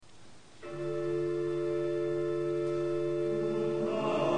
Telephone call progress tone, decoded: dial tone